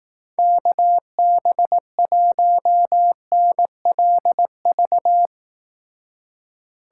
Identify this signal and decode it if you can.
Morse code, heard as a keyed tone, about 19 wpm, 700 Hz. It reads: KB1NLV
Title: Speed Racing - USA Call Signs 15wpm